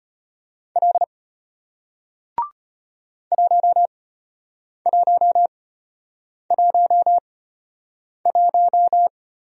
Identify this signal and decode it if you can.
Morse code, heard as a keyed tone, about 34 wpm, spaced as for 8 wpm, 700 Hz. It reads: L 1111